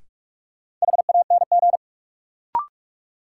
Morse code, heard as a keyed tone, about 44 wpm, 700 Hz. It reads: HANG
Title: Sentences from Top 1600 Words - New Word Drill Alt - 45wpm